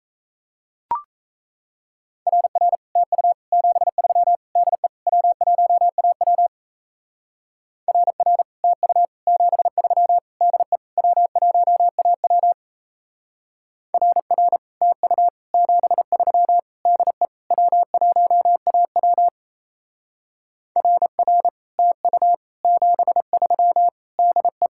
Morse code, heard as a keyed tone, about 35 wpm, 700 Hz. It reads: RR TU 73 DE W1AW RR TU 73 DE W1AW RR TU 73 DE W1AW RR TU 73 DE
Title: Speed-Racing Common QSO Phrases - 28wpm